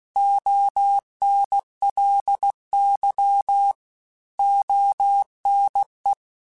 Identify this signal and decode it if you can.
Morse code, heard as a keyed tone, about 16 wpm, 800 Hz. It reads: ONLY ONE
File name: Technical Subjects / Technical Subjects-mp3s / AND Gate @15